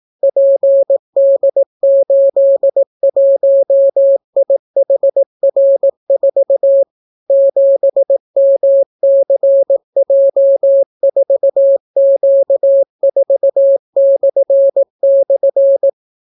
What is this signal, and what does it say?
Morse code, heard as a keyed tone, about 18 wpm, 550 Hz. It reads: PD81IHR4 7MCJ4Q4//